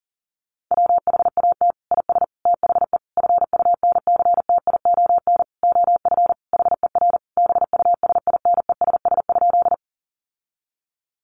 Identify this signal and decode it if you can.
Morse code, heard as a keyed tone, about 40 wpm, 700 Hz. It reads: WHAT IS THE FUNCTION OF HER BUSINESS?
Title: Sentences from Top 1100 Words 40wpm